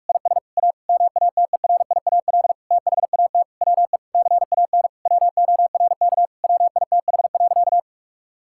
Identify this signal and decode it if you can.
Morse code, heard as a keyed tone, about 46 wpm, 700 Hz. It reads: IS A MATERIAL THAT WE CAN WORK WITH.
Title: Sentences from Top 1000 Words - Review Part 5 - 45wpm